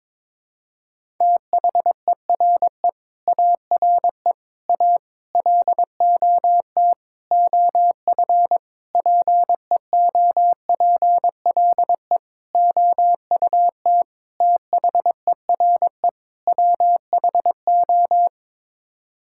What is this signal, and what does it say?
Morse code, heard as a keyed tone, about 22 wpm, 700 Hz. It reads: THERE ARE A LOT OF PEOPLE OUT THERE WHO
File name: Sets of 9 Words - 22wpm